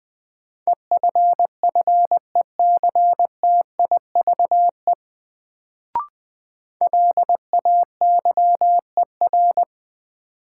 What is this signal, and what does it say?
Morse code, heard as a keyed tone, about 20 wpm, 700 Hz. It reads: EFFECTIVE LAYER
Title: Sentences from Top 1600 Words - New Word Drill Alt - 20wpm